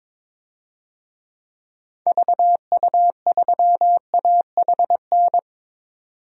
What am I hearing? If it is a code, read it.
Morse code, heard as a keyed tone, about 22 wpm, 700 Hz. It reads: VU3AHN
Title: DXCC-100 Call Signs - 6 Characters - 22wpm